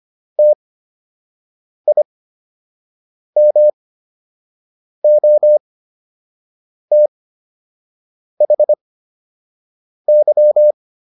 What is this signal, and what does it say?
Morse code, heard as a keyed tone, about 25 wpm, spaced as for 6 wpm, 600 Hz. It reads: TIMOTHY